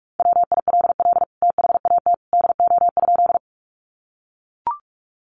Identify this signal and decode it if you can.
Morse code, heard as a keyed tone, about 44 wpm, 700 Hz. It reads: WILL THAT DO?
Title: Sentences from Top 100 Words 45wpm